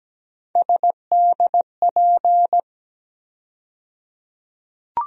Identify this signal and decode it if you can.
Morse code, heard as a keyed tone, about 17 wpm, 700 Hz. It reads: SDP
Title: USA Call Sign Postfixes - 3 Letters 17wpm